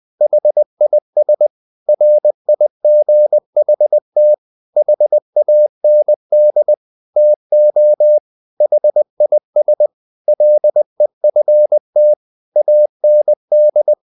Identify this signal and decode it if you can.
Morse code, heard as a keyed tone, about 20 wpm, 600 Hz. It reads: HIS RIGHT HAND TO HIS LEFT AND
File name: the old man and the sea part 2 20wpm